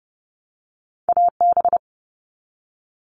Morse code, heard as a keyed tone, about 30 wpm, 700 Hz. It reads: AB
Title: Canadian Provinces and Territories - Repeated 3x - 30wpm